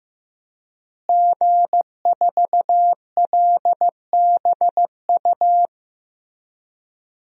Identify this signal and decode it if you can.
Morse code, heard as a keyed tone, about 15 wpm, 700 Hz. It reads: G4LBU
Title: DXCC-100 Call Signs - 5 Characters - Repeated 3x - 15wpm